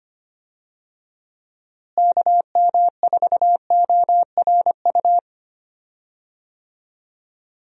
Morse code, heard as a keyed tone, about 25 wpm, 700 Hz. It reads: KM4ORU
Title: USA Call Signs - 6 Characters - Repeated 2x - 25wpm